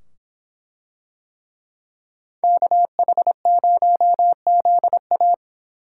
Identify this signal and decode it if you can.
Morse code, heard as a keyed tone, about 25 wpm, 700 Hz. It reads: KH0ZA